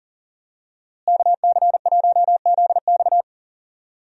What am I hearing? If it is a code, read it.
Morse code, heard as a keyed tone, about 40 wpm, 700 Hz. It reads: KC1ZX
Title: USA Call Signs - 5 Characters - Repeated 2x - 40wpm